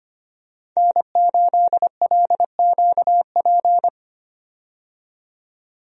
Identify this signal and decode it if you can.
Morse code, heard as a keyed tone, about 25 wpm, 700 Hz. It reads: N8LQP
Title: USA Call Signs - 5 Characters - 25wpm